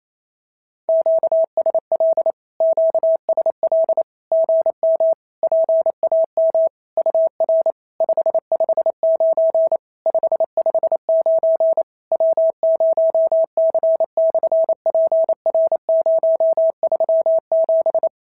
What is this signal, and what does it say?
Morse code, heard as a keyed tone, about 28 wpm, 650 Hz. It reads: QSL QSL GM PAM UR 559 559 W0C/PR037
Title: SOTA QSO Practice 28wpm